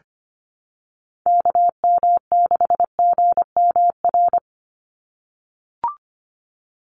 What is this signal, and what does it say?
Morse code, heard as a keyed tone, about 25 wpm, 700 Hz. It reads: KM6GMR
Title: USA Call Signs 25wpm